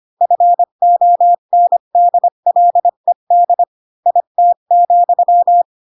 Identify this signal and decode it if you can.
Morse code, heard as a keyed tone, about 25 wpm, 700 Hz. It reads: FONDLED IT,
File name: Pinocchio / Pinocchio ch5 20 25 ik1hge